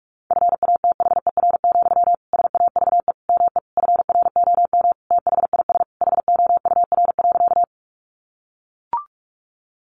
Morse code, heard as a keyed tone, about 44 wpm, 700 Hz. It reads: FATHER, SAVE ME FROM THIS HOUR.
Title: Sentences from Top 1100 Words 45wpm